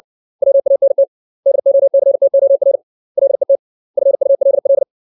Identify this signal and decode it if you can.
Morse code, heard as a keyed tone, about 60 wpm, 550 Hz. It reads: WANT DOCTOR LET FULL
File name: 632 words / 632 Words 60WPM